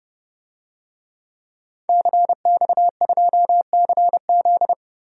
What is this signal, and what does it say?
Morse code, heard as a keyed tone, about 32 wpm, 700 Hz. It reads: CX2CZ
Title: DXCC-100 Call Signs - 5 Characters - Repeated 3x - 30wpm